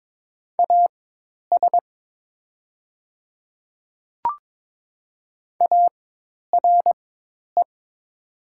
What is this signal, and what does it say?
Morse code, heard as a keyed tone, about 23 wpm, spaced as for 10 wpm, 700 Hz. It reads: AS ARE
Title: 4x Character Spacing - Top 100 Words - 22wpm